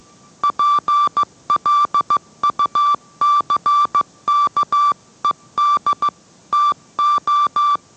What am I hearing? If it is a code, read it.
Morse code, heard as a keyed tone, about 15 wpm, 1170 Hz. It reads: PLUCKED TO